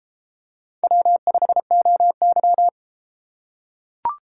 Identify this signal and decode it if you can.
Morse code, heard as a keyed tone, about 32 wpm, 700 Hz. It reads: W5OY